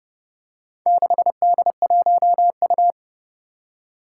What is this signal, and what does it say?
Morse code, heard as a keyed tone, about 30 wpm, 700 Hz. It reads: 6D1U
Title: Four Challenging Letter-Number Combinations 30wpm